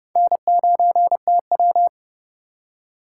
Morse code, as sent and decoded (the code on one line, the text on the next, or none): -. ----. - .--
N9TW